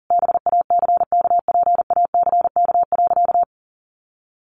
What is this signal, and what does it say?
Morse code, heard as a keyed tone, about 40 wpm, 700 Hz. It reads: BACKPACK.
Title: Additional Easy Phrases 40wpm